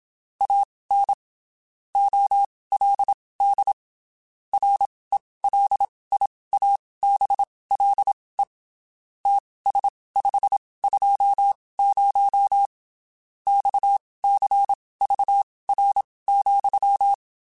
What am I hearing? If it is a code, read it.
Morse code, heard as a keyed tone, about 27 wpm, spaced as for 19 wpm, 800 Hz. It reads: AN OLD RELIABLE TS520 XCVR,
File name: (spaced) / QSOs Spaced-mp3s / WD0FBT 1 @25 sp